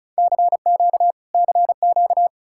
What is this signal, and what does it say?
Morse code, heard as a keyed tone, about 47 wpm, spaced as for 41 wpm, 700 Hz. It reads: CQ CQ
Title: Speed-Racing Common QSO Phrases - 35wpm